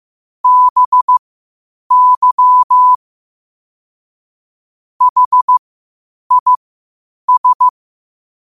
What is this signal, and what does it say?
Morse code, heard as a keyed tone, about 15 wpm, spaced as for 8 wpm, 1000 Hz. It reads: BY HIS
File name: qotdfiles / 8wpm